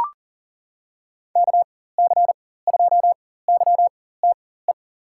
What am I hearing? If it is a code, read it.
Morse code, heard as a keyed tone, about 40 wpm, spaced as for 19 wpm, 700 Hz. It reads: KC2YTE
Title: USA Call Signs - 6 Characters - 4x Character Spacing - 40wpm